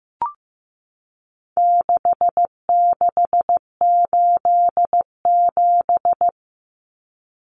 Morse code, heard as a keyed tone, about 15 wpm, 700 Hz. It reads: 6687
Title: Common Cut Numbers - 4-Digit 15wpm